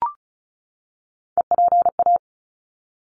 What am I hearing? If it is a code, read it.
Morse code, heard as a keyed tone, about 34 wpm, 700 Hz. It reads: EPA
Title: ARRL Sections 35wpm